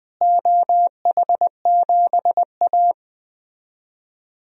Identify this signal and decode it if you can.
Morse code, heard as a keyed tone, about 20 wpm, 700 Hz. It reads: OH7A
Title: DXCC-100 Call Signs - 4 Characters - Repeated 3x - 20wpm